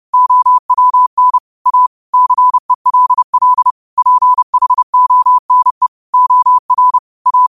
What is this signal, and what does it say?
Morse code, heard as a keyed tone, about 30 wpm, 1000 Hz. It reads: OWN A CELL PHONE OR A
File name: qotdfiles / 30wpm